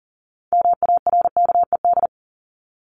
Morse code, heard as a keyed tone, about 40 wpm, 700 Hz. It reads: MARKED
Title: ED Suffix - Top 100 Words with Base Words 40wpm